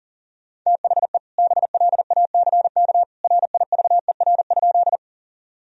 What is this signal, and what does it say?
Morse code, heard as a keyed tone, about 37 wpm, 700 Hz. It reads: THE BLACK RIVER?
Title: Sentences from Top 1000 Words - Review Part 2 - 40wpm